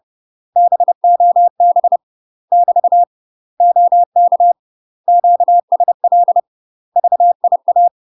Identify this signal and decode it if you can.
Morse code, heard as a keyed tone, about 30 wpm, 700 Hz. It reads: BOB <BT> OK QSL VIA
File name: UK MASTER QSOs TRAINING / 30WPM / UK MASTER QSOs 0007